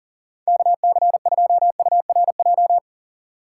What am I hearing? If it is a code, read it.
Morse code, heard as a keyed tone, about 40 wpm, 700 Hz. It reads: KC2URJ
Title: USA Call Signs - 6x Character Spacing - 40wpm